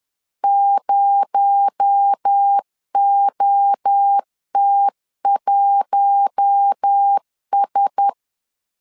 Morse code, heard as a keyed tone, about 10 wpm, 800 Hz. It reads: 0OT1S
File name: LEZIONI / 8 YJ45 / 8 YJ45 10wpm 2549